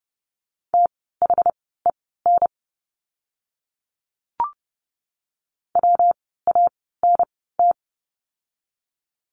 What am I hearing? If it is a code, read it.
Morse code, heard as a keyed tone, about 30 wpm, spaced as for 17 wpm, 700 Hz. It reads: THEN WANT
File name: course / 30:10wpm / 237 Lesson 38f - Letter q - Words 30wpm